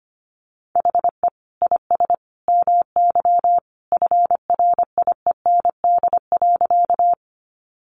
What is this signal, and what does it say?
Morse code, heard as a keyed tone, about 25 wpm, 700 Hz. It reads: HE IS MY FRIEND.
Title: Sentences from Top 300 Words 25wpm